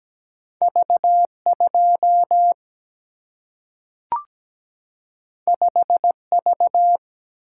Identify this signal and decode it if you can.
Morse code, heard as a keyed tone, about 17 wpm, 700 Hz. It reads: V2 5V
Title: Single-Letter Single-Number 17wpm